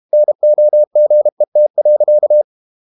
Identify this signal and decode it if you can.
Morse code, heard as a keyed tone, about 32 wpm, 600 Hz. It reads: NOGET.
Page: Ferie 32 wpm